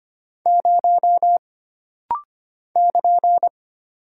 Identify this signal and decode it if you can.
Morse code, heard as a keyed tone, about 25 wpm, 700 Hz. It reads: 0 <KN>
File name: Single Character (Letter:Number:Punctuation:Prosign) - Rapid-Fire - 25wpm